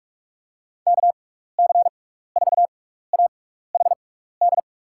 Morse code, heard as a keyed tone, about 46 wpm, spaced as for 15 wpm, 700 Hz. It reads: KC4AHD
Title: USA Call Signs - 6 Characters - 6x Character Spacing - 45wpm